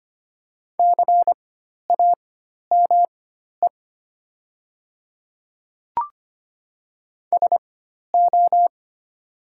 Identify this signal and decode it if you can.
Morse code, heard as a keyed tone, about 25 wpm, spaced as for 12 wpm, 700 Hz. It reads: CAME SO
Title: 4x Character Spacing - Top 200 Words - 25wpm